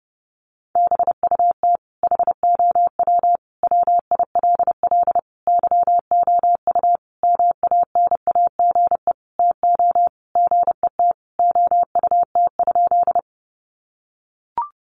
Morse code, heard as a keyed tone, about 30 wpm, 700 Hz. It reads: BUT HOW WILL YOU MANAGE TO GET OUT?
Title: Sentences from Top 1400 Words 30wpm